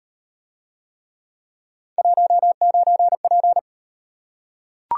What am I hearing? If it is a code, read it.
Morse code, heard as a keyed tone, about 37 wpm, 700 Hz. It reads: 19P